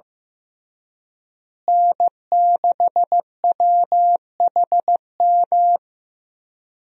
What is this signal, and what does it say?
Morse code, heard as a keyed tone, about 15 wpm, 700 Hz. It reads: N6WHM
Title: USA Call Signs - 5 Characters - Repeated 2x - 15wpm